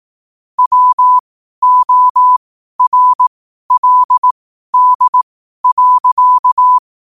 Morse code, heard as a keyed tone, about 18 wpm, spaced as for 12 wpm, 1000 Hz. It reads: WORLD.
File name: qotdfiles / 12wpm